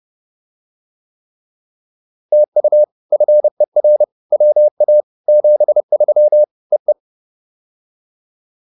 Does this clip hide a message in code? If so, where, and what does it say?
Yes: Morse code from 2.3 to 6.9 s, heard as a keyed tone, about 30 wpm, 590 Hz. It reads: TU FER WA 73 E E